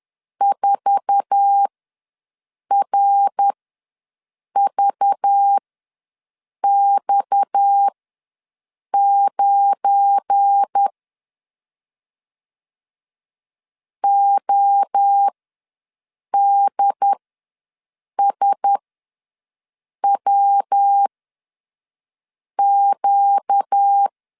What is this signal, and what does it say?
Morse code, heard as a keyed tone, about 10 wpm, spaced as for 6 wpm, 800 Hz. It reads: 4RVX9 ODSWQ
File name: LEZIONI / 8 YJ45 / 8 YJ4 ws 10wpm 4020